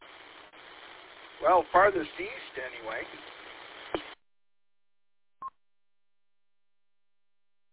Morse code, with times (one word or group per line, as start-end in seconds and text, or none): none